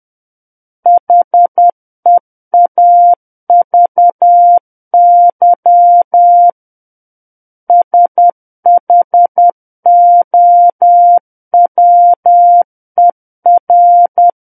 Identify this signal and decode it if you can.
Morse code, heard as a keyed tone, about 10 wpm, 700 Hz. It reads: HEAVY SHOWER
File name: Book One / 01-Chapter 1 - 10 WPM